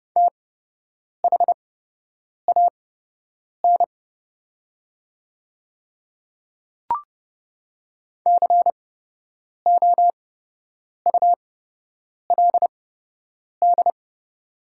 Morse code, heard as a keyed tone, about 30 wpm, spaced as for 8 wpm, 700 Hz. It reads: THAN COULD
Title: 8x Character Spacing - Top 100 Words - 30wpm